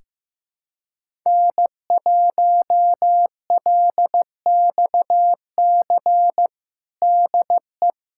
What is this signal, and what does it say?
Morse code, heard as a keyed tone, about 16 wpm, 700 Hz. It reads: N1LXC DE
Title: USA Callsign DE Callsign - 5 Characters - 15wpm